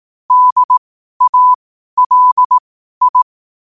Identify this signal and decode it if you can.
Morse code, heard as a keyed tone, about 18 wpm, spaced as for 12 wpm, 1000 Hz. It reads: DALI
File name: qotdfiles / 12wpm